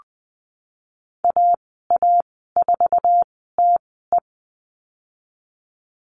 Morse code, heard as a keyed tone, about 20 wpm, spaced as for 14 wpm, 700 Hz. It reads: AA4TE
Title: 062 Lesson 11g - Letter h - Call Signs 20wpm